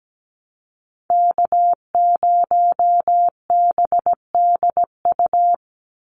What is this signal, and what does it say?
Morse code, heard as a keyed tone, about 17 wpm, 700 Hz. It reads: K0BDU
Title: USA Call Signs 17wpm